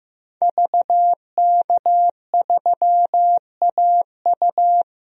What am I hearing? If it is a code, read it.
Morse code, heard as a keyed tone, about 15 wpm, 700 Hz. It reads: VK3AU
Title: DXCC-100 Call Signs - 5 Characters - 15wpm